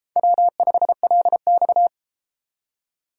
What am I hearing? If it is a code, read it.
Morse code, heard as a keyed tone, about 31 wpm, 700 Hz. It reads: W5LX